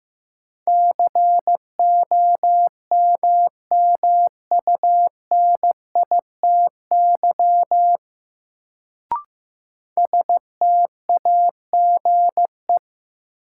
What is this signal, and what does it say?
Morse code, heard as a keyed tone, about 15 wpm, 700 Hz. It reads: COMMUNITY STAGE
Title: Sentences from Top 1200 Words - New Word Drill Alt - 15wpm